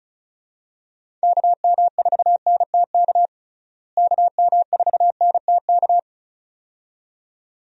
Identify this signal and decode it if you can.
Morse code, heard as a keyed tone, about 36 wpm, 700 Hz. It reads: KM4NTK KM4NTK
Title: USA Call Signs - 6 Characters - Repeated 2x - 35wpm